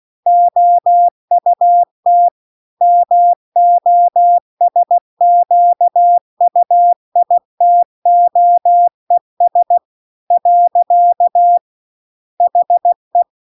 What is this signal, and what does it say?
Morse code, heard as a keyed tone, about 16 wpm, 700 Hz. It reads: OUT MOSQUITOES . HE